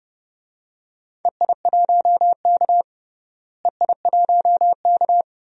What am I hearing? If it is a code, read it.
Morse code, heard as a keyed tone, about 31 wpm, 700 Hz. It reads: EI1K EI1K
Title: DXCC-100 Call Signs - 4 Characters - Repeated 2x - 30wpm